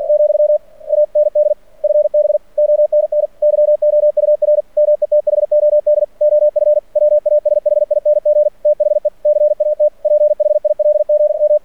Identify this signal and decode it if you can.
Morse code, heard as a keyed tone, about 49 wpm, 600 Hz. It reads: , AND FB ONN YOUR METHOD OF WASHING THE CAT PHILGM
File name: archived / CRSnet